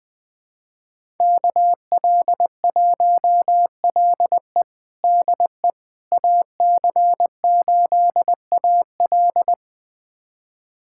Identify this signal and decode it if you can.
Morse code, heard as a keyed tone, about 21 wpm, 700 Hz. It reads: KL1LE DE AC8AL
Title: USA Callsign DE Callsign - 5 Characters - 20wpm